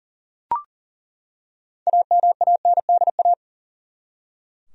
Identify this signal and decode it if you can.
Morse code, heard as a keyed tone, about 38 wpm, 700 Hz. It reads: AMANDA
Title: USA Common Names - 40wpm